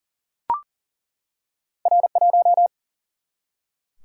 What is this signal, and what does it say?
Morse code, heard as a keyed tone, about 42 wpm, 700 Hz. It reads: R1
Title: Single-Letter Single-Number 40wpm